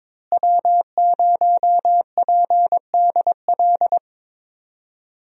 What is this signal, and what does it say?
Morse code, heard as a keyed tone, about 22 wpm, 700 Hz. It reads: W0PDL
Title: USA Call Signs - 5 Characters - Repeated 3x - 22wpm